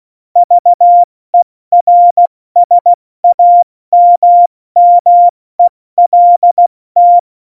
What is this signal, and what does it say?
Morse code, heard as a keyed tone, about 16 wpm, spaced as for 14 wpm, 700 Hz. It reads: VERSAMMELT
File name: Morsekurs / Witze / Witz 27. Februar 2018, 70 F80